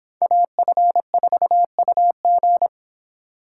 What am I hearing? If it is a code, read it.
Morse code, heard as a keyed tone, about 26 wpm, 700 Hz. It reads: AF4UG